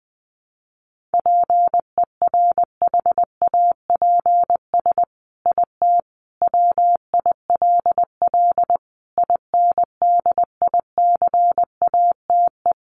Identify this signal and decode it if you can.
Morse code, heard as a keyed tone, about 19 wpm, 700 Hz. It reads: PERHAPS IT WILL INDICATE
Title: Sentences from Top 1300 Words 20wpm